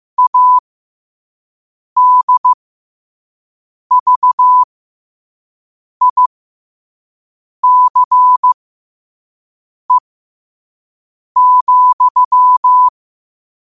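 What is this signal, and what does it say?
Morse code, heard as a keyed tone, about 15 wpm, spaced as for 5 wpm, 1000 Hz. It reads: ADVICE,
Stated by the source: 5 wpm Morse code practice recording.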